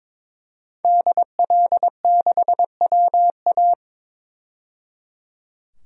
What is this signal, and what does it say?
Morse code, heard as a keyed tone, about 22 wpm, 700 Hz. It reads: DL6WA
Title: DXCC-100 Call Signs - 5 Characters - 22wpm